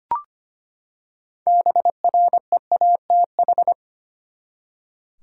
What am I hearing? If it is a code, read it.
Morse code, heard as a keyed tone, about 25 wpm, 700 Hz. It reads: BREATH
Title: Six Letter Words - 25wpm